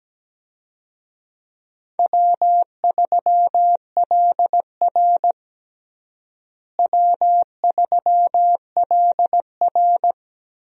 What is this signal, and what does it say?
Morse code, heard as a keyed tone, about 17 wpm, 700 Hz. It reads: W3LR W3LR
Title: USA Call Signs - 4 Characters - Repeated 3x - 17wpm